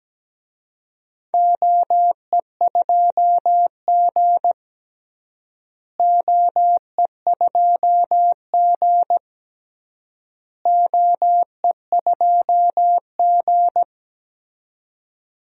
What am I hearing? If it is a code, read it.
Morse code, heard as a keyed tone, about 17 wpm, 700 Hz. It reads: OE2G OE2G OE2G
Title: DXCC-100 Call Signs - 4 Characters - Repeated 3x - 17wpm